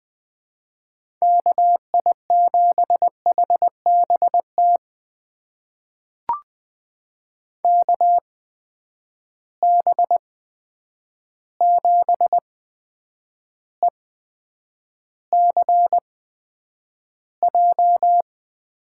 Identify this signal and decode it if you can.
Morse code, heard as a keyed tone, about 20 wpm, spaced as for 9 wpm, 700 Hz. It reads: KI7HBT KB7ECJ
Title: USA Call Signs - 6 Characters - 8x Character Spacing - 20wpm